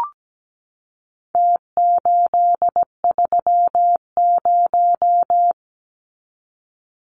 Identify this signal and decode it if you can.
Morse code, heard as a keyed tone, about 17 wpm, 700 Hz. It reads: T830
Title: Numbers - Contest Style - 4-Digit 17wpm